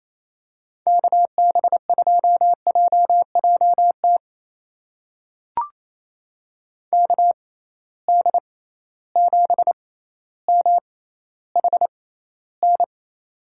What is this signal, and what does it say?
Morse code, heard as a keyed tone, about 28 wpm, spaced as for 14 wpm, 700 Hz. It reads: KB2JJT KD7MHN
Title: USA Call Signs - 6 Characters - 6x Character Spacing - 28wpm